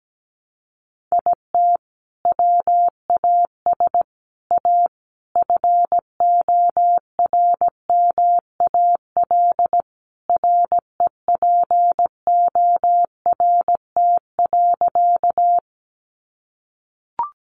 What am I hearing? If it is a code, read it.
Morse code, heard as a keyed tone, about 17 wpm, 700 Hz. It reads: IT WAS A FORMAL REPORT.
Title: Sentences from Top 1300 Words 17wpm